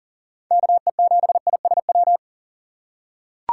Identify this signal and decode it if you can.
Morse code, heard as a keyed tone, about 37 wpm, 700 Hz. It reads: KE7ISW
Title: USA Call Signs - 6x Character Spacing - 40wpm